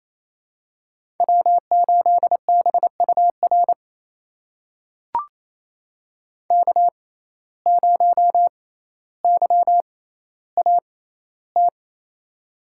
Morse code, heard as a keyed tone, about 28 wpm, spaced as for 14 wpm, 700 Hz. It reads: W8BUR K0YAT
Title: USA Call Signs - 5 Characters - 6x Character Spacing - 28wpm